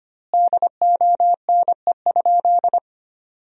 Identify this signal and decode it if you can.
Morse code, heard as a keyed tone, about 25 wpm, 700 Hz. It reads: DONE?
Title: 2x Word-Spacing - Sentences from Top 500 Words - 25wpm